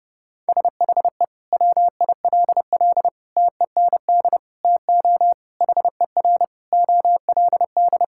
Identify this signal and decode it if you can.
Morse code, heard as a keyed tone, about 30 wpm, 700 Hz. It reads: SHE WILL TEND TO HER OLD
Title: Sentences from Top 1000 Words - Review Part 5 - 30wpm